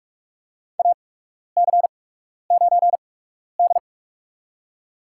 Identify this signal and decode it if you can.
Morse code, heard as a keyed tone, about 46 wpm, spaced as for 12 wpm, 700 Hz. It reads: AC9D
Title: USA Call Signs - 4 Characters - 8x Character Spacing - 45wpm